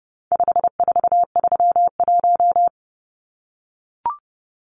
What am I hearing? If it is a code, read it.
Morse code, heard as a keyed tone, about 30 wpm, 700 Hz. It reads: 5431
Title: Numbers - 4-Digit 30wpm